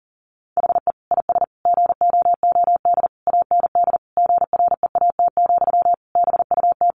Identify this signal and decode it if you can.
Morse code, heard as a keyed tone, about 40 wpm, 700 Hz. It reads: HE IS GOOD AND GREAT, BUT
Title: Sentences from Top 1400 Words 40wpm